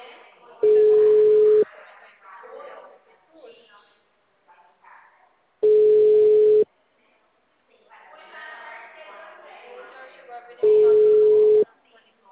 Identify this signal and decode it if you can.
Ringback tone (Europe, 425 Hz), with speech.